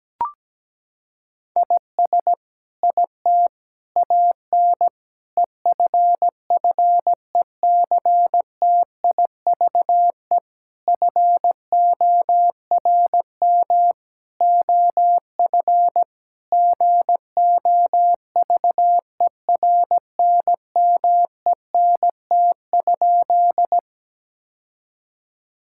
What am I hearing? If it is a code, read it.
Morse code, heard as a keyed tone, about 17 wpm, 700 Hz. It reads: IS IT AN EFFECTIVE FORM OF GOVERNMENT?
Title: Sentences from Top 1000 Words - Review Part 6 - 17wpm